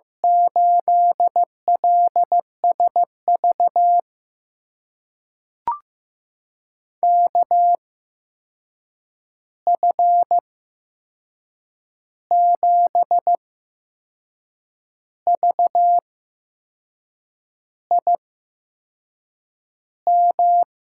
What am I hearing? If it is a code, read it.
Morse code, heard as a keyed tone, about 15 wpm, spaced as for 6 wpm, 700 Hz. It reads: 8LSV KF7VIM
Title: USA Call Signs - 6 Characters - 8x Character Spacing - 15wpm